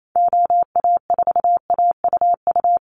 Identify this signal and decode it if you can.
Morse code, heard as a keyed tone, about 28 wpm, 700 Hz. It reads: OA4AUU
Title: USA Call Signs Repeated 2x 28wpm